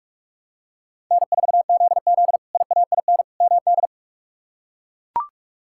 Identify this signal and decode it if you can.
Morse code, heard as a keyed tone, about 42 wpm, 700 Hz. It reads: N4ZZ IAIN MD
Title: K1USN SST QSO Practice - 45wpm